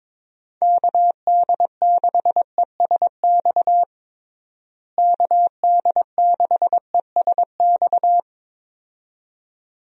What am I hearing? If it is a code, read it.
Morse code, heard as a keyed tone, about 21 wpm, 700 Hz. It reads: KD6ESX KD6ESX
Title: USA Call Signs - 6 Characters - Repeated 2x - 22wpm